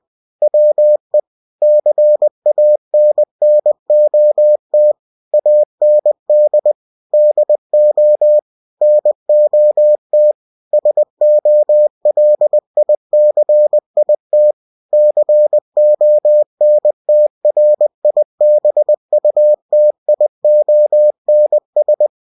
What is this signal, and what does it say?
Morse code, heard as a keyed tone, about 20 wpm, 600 Hz. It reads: WE CANNOT AND DO NOT SOLICIT CONTRIBUTIONS